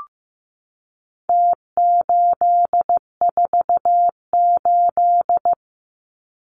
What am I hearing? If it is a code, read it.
Morse code, heard as a keyed tone, about 15 wpm, 700 Hz. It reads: T848
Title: Numbers - Contest Style - 4-Digit 15wpm